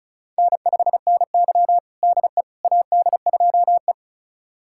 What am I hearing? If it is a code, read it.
Morse code, heard as a keyed tone, about 36 wpm, 700 Hz. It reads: N5NY DE AD2E
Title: USA Callsign DE Callsign - 4 Characters - 35wpm